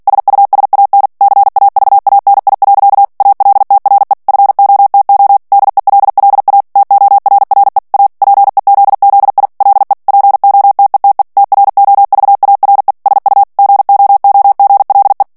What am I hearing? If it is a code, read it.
Morse code, heard as a keyed tone, about 48 wpm, 800 Hz. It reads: SUSAN CAVANI. ALTRE FOTO DELLA TORRE A PEZZI LE POTETE TROVARE SU GOOGLE
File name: morse mp3 / ITALIEN texte clair 1 vitesse 45 mots minute